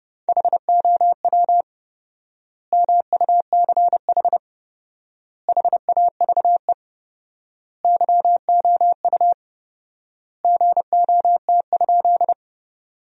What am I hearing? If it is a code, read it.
Morse code, heard as a keyed tone, about 30 wpm, 700 Hz. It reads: HOW MUCH HAVE YOU GOT?
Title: 4x Word-Spacing - Sentences from Top 300 Words - 30wpm